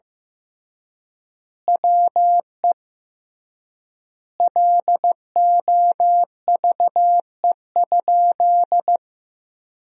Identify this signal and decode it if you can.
Morse code, heard as a keyed tone, about 14 wpm, 700 Hz. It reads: WE LOVE?
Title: 3x Word-Spacing - Sentences from Top 400 Words - 15wpm